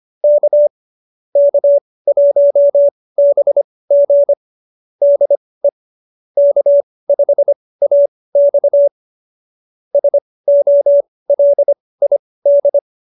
Morse code, heard as a keyed tone, about 25 wpm, spaced as for 18 wpm, 570 Hz. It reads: K K1BG DE K5AX SOLID
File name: qso201b 18